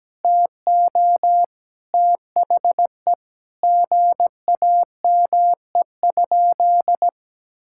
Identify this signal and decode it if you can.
Morse code, heard as a keyed tone, about 17 wpm, 700 Hz. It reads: TO THE GAME?
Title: Sentences from Top 1600 Words 17wpm